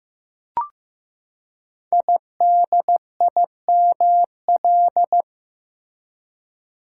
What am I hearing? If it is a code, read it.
Morse code, heard as a keyed tone, about 15 wpm, 700 Hz. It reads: IDIML
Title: Code Groups - 5 Letter:Numbers - 15wpm